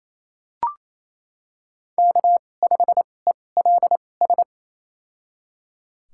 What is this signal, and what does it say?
Morse code, heard as a keyed tone, about 28 wpm, spaced as for 20 wpm, 700 Hz. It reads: K5ELS
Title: USA Call Signs - 5 Characters - 2x Character Spacing - 28wpm